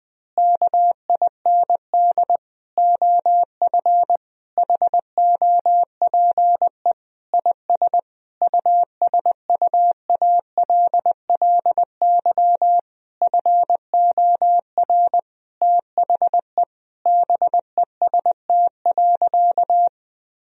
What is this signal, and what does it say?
Morse code, heard as a keyed tone, about 20 wpm, 700 Hz. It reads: KIND OF HOPE IS USUALLY FOR THE BEST.
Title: Sentences from Top 1000 Words - Review Part 3 - 20wpm